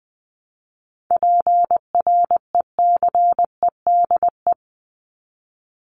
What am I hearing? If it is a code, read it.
Morse code, heard as a keyed tone, about 20 wpm, 700 Hz. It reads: PRECEDE
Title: ING Suffix - Top 100 Words with Base Words 20wpm